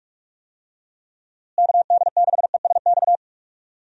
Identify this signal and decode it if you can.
Morse code, heard as a keyed tone, about 42 wpm, 700 Hz. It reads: KD6ESX
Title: USA Call Signs - 6 Characters - Repeated 2x - 45wpm